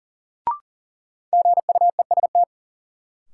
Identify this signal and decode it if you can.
Morse code, heard as a keyed tone, about 37 wpm, 700 Hz. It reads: GUEST
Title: Sentences from Top 1600 Words - New Word Drill - 40wpm